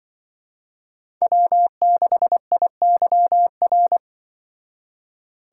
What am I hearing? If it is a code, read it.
Morse code, heard as a keyed tone, about 25 wpm, 700 Hz. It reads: W6IYR
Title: Speed Racing - USA 1x3 Call Signs 20wpm